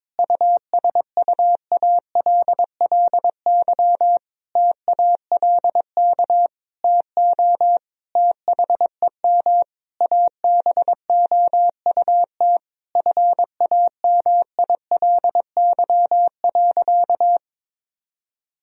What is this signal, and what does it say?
Morse code, heard as a keyed tone, about 22 wpm, 700 Hz. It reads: USUALLY TALK TO THEM ABOUT FAMILY.
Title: Sentences from Top 1000 Words - Review Part 3 - 22wpm